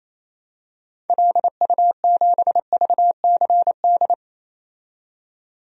Morse code, heard as a keyed tone, about 28 wpm, 700 Hz. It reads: LU7VCD
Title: DXCC-100 Call Signs - 6 Characters - 28wpm